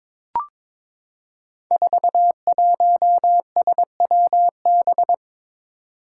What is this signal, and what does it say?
Morse code, heard as a keyed tone, about 22 wpm, 700 Hz. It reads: 41SWB
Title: Five Challenging Letter-Number Combinations 22wpm